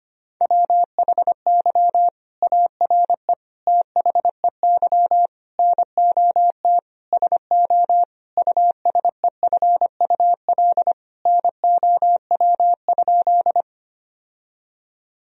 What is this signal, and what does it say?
Morse code, heard as a keyed tone, about 25 wpm, 700 Hz. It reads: WHY ARE THEY NOT SO USEFUL NOW?
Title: Sentences from Top 1000 Words - Review Part 4 - 25wpm